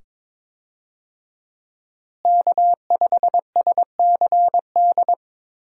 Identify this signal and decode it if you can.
Morse code, heard as a keyed tone, about 22 wpm, 700 Hz. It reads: K5SCD
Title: USA Call Signs - 5 Characters - Repeated 2x - 22wpm